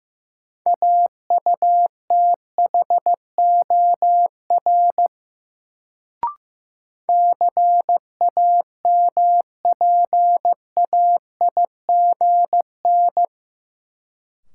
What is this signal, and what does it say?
Morse code, heard as a keyed tone, about 15 wpm, 700 Hz. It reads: AUTHOR CAMPAIGN
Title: Sentences from Top 1200 Words - New Word Drill Alt - 15wpm